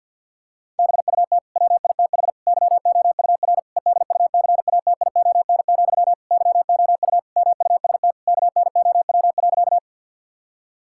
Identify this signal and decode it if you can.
Morse code, heard as a keyed tone, about 50 wpm, 700 Hz. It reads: BUT WITH YOUR EDUCATION, YOU MUST KNOW.
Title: Sentences from Top 1000 Words - Review Part 6 - 50wpm